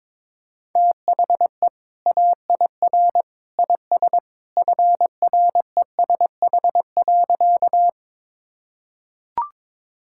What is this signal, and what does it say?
Morse code, heard as a keyed tone, about 22 wpm, 700 Hz. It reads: THE AIR IS FRESH.
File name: practice / 22wpm / Sentences with 2-Word Phrases - 22wpm